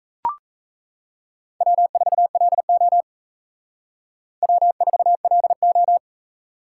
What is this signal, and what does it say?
Morse code, heard as a keyed tone, about 40 wpm, 700 Hz. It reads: W4LO W4LO